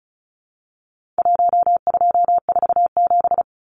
Morse code, heard as a keyed tone, about 34 wpm, 700 Hz. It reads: 1247
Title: Numbers - 4-Digit 35wpm